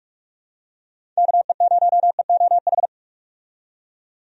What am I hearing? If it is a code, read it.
Morse code, heard as a keyed tone, about 43 wpm, 700 Hz. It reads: KE0EOH
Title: USA Call Signs - 6 Characters - 45wpm